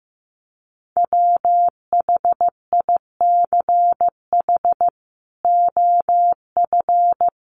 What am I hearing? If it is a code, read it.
Morse code, heard as a keyed tone, about 15 wpm, 700 Hz. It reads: WHICH OF
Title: Sentences from Top 1100 Words 15wpm